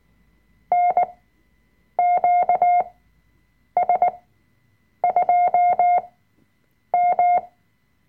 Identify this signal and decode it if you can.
Morse code, heard as a keyed tone, about 19 wpm, spaced as for 7 wpm, 710 Hz. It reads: NQS2M
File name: Workshop Dateien / CW-Online-Kurs / Lektion 005